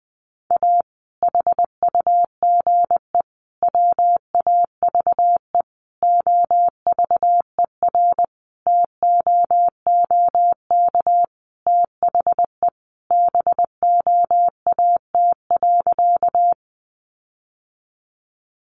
Morse code, heard as a keyed tone, about 20 wpm, 700 Hz. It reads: A HUGE WAVE OVERTOOK THE BOAT.
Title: Sentences from Top 1400 Words 20wpm